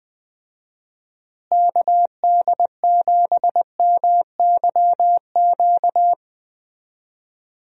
Morse code, heard as a keyed tone, about 21 wpm, 700 Hz. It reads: KD7MYQ
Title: USA Call Signs - 6 Characters - Repeated 3x - 20wpm